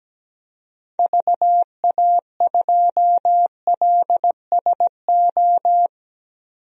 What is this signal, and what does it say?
Morse code, heard as a keyed tone, about 17 wpm, 700 Hz. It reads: VA2LSO
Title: DXCC-100 Call Signs - 6 Characters - 17wpm